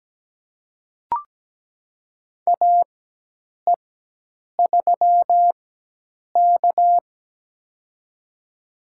Morse code, heard as a keyed tone, about 17 wpm, spaced as for 8 wpm, 700 Hz. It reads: AE3K
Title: USA Call Signs - 4x Character Spacing - 17wpm